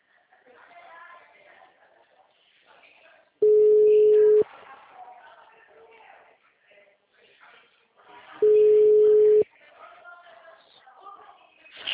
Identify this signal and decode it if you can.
Ringback tone (Europe, 425 Hz), with speech.
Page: Atibaia - SP 1